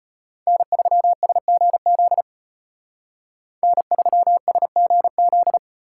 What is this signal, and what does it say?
Morse code, heard as a keyed tone, about 31 wpm, 700 Hz. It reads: N3SGZ N3SGZ
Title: Speed Racing - USA 1x3 Call Signs 25wpm